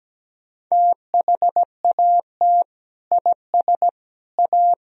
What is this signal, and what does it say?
Morse code, heard as a keyed tone, about 17 wpm, 700 Hz. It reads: THAT IS A
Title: Sentences from Top 1000 Words - Review Part 3 - 17wpm